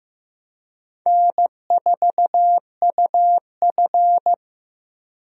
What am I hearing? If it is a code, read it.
Morse code, heard as a keyed tone, about 15 wpm, 700 Hz. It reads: N4UF
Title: USA Call Signs - 4 Characters - Repeated 2x - 15wpm